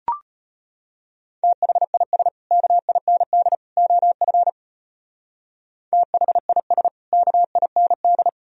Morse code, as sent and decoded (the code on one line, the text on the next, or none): - .... .. ... / -.- .. -. -.. / --- ..-. / - .... .. ... / -.- .. -. -..
THIS KIND OF THIS KIND